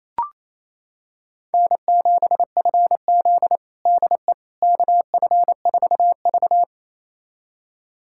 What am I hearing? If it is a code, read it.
Morse code, heard as a keyed tone, about 28 wpm, 700 Hz. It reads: N7FZ DE KF4V
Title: USA Callsign DE Callsign - 4 Characters - 28wpm